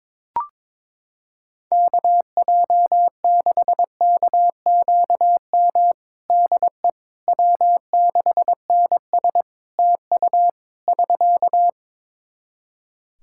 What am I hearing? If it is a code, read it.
Morse code, heard as a keyed tone, about 22 wpm, 700 Hz. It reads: KJ6KQM DE W6NS TU <SK>
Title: Practice QSO - Exchange 4 - Ending - 22wpm